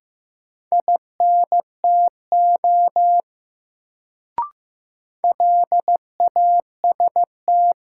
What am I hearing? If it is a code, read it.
Morse code, heard as a keyed tone, about 15 wpm, 700 Hz. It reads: INTO LAST
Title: Sets of 2 Words - New Word Drill - 15wpm